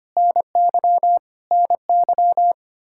Morse code, heard as a keyed tone, about 25 wpm, 700 Hz. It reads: NY NY
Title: Practice QSO - Exchange 1 - Beginning - 25wpm